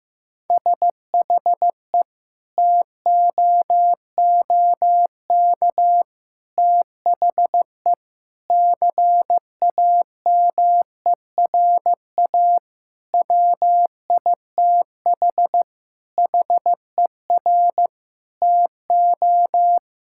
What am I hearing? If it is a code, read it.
Morse code, heard as a keyed tone, about 15 wpm, 700 Hz. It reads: SHE TOOK THE CAMERA WITH HER TO
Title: Sentences from Top 1000 Words - Review Part 5 - 15wpm